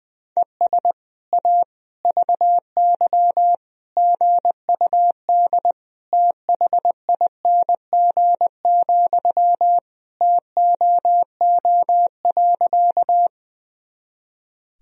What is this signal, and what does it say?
Morse code, heard as a keyed tone, about 21 wpm, 700 Hz. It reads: ES A VY GUD THING, TOO.
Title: Sentences from Top 200 Words and Abbreviations 20wpm